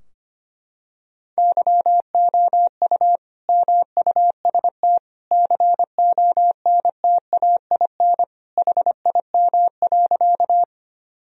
Morse code, heard as a keyed tone, about 25 wpm, 700 Hz. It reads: YOU MUST CONTAIN HIM.
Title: Sentences from Top 1000 Words - Review Part 2 - 25wpm